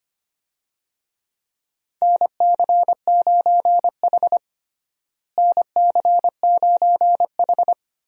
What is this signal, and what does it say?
Morse code, heard as a keyed tone, about 25 wpm, 700 Hz. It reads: NC9H NC9H
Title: USA Call Signs - 4 Characters - Repeated 3x - 25wpm